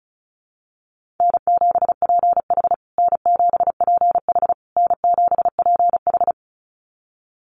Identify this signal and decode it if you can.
Morse code, heard as a keyed tone, about 34 wpm, 700 Hz. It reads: N7PH N7PH N7PH
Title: USA Call Signs Repeated 3x 35wpm